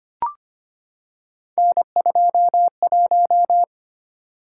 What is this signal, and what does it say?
Morse code, heard as a keyed tone, about 25 wpm, 700 Hz. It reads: N21
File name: Three Letter-Number Combinations - 25wpm